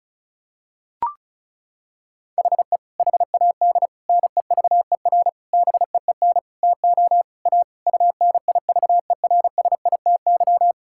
Morse code, heard as a keyed tone, about 36 wpm, 700 Hz. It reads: HE HAD NEVER BEEN TO A UNIVERSITY
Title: Sentences from Top 1000 Words - Review Part 5 - 35wpm